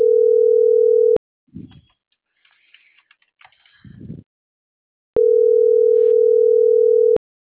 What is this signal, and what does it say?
Ringback tone (North America), 440 + 480 Hz.